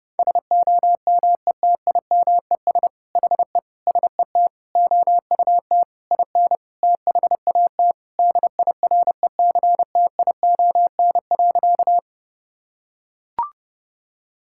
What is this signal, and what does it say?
Morse code, heard as a keyed tone, about 30 wpm, 700 Hz. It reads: SOMETIMES HE SET OUT IN THAT DIRECTION.
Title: Sentences from Top 1000 Words - Review Part 5 - 30wpm